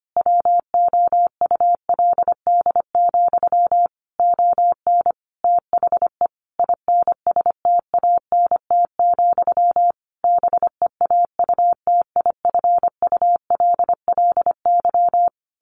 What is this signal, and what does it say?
Morse code, heard as a keyed tone, about 25 wpm, 700 Hz. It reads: WOULD, ON THE INSTANT, BEAUTIFULLY
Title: A Sound Of Thunder 25wpm